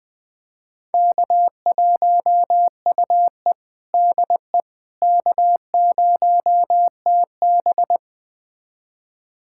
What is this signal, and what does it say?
Morse code, heard as a keyed tone, about 21 wpm, 700 Hz. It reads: K1UE DE K0TB
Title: USA Callsign DE Callsign - 4 Characters - 20wpm